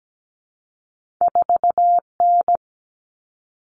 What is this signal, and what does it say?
Morse code, heard as a keyed tone, about 17 wpm, 700 Hz. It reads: 4N
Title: All Cut Numbers - 2-Digit 17wpm